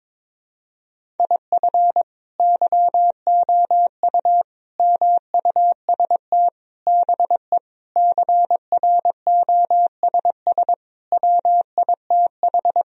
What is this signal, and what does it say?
Morse code, heard as a keyed tone, about 22 wpm, 700 Hz. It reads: IF YOU MUST BE CROSS WITH
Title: Sentences from Top 1000 Words - Review Part 1 - 22wpm